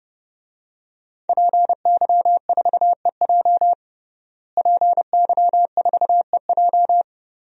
Morse code, heard as a keyed tone, about 30 wpm, 700 Hz. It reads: PY4EJ PY4EJ
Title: DXCC-100 Call Signs - 5 Characters - Repeated 2x - 30wpm